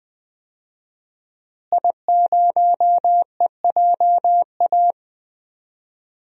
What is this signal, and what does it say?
Morse code, heard as a keyed tone, about 20 wpm, 700 Hz. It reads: I0EJA
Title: DXCC-100 Call Signs - 5 Characters - Repeated 3x - 20wpm